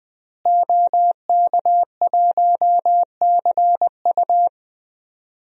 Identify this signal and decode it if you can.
Morse code, heard as a keyed tone, about 20 wpm, 700 Hz. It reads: OK1CU
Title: DXCC-100 Call Signs - 5 Characters - Repeated 2x - 20wpm